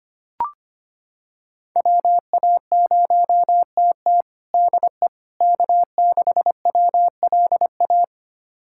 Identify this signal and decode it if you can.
Morse code, heard as a keyed tone, about 25 wpm, 700 Hz. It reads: WA0TT DE K6WLA
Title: USA Callsign DE Callsign - 5 Characters - 25wpm